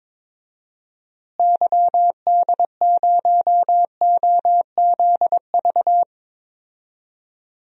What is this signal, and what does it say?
Morse code, heard as a keyed tone, about 22 wpm, 700 Hz. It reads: YD0OZV
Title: DXCC-100 Call Signs - 6 Characters - Repeated 2x - 22wpm